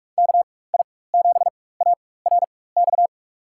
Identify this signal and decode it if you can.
Morse code, heard as a keyed tone, about 46 wpm, spaced as for 21 wpm, 700 Hz. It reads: KI7ARX
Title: USA Call Signs - 6 Characters - 4x Character Spacing - 45wpm